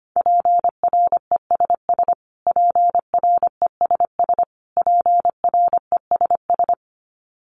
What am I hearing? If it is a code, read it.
Morse code, heard as a keyed tone, about 25 wpm, 700 Hz. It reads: PRESS PRESS PRESS
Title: Top 300 Words Repeated 25wpm